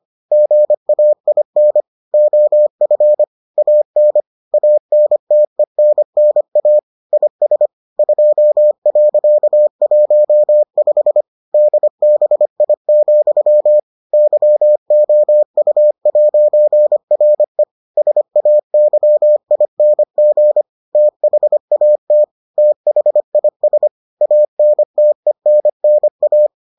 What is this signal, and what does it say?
Morse code, heard as a keyed tone, about 25 wpm, 600 Hz. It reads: GAIN OF AN ANTENNA IS 2.15 DBI, YOU'RE SAYING THAT THIS ANTENNA